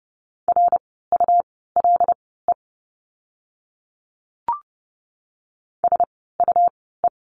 Morse code, heard as a keyed tone, about 30 wpm, spaced as for 17 wpm, 700 Hz. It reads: RULE SUE
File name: Lesson 17f - Letter u - Words 30wpm